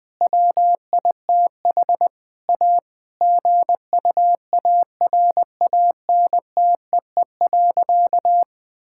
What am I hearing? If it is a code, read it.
Morse code, heard as a keyed tone, about 20 wpm, 700 Hz. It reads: WITH A GUARANTEE.
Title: Sentences from Top 1600 Words 20wpm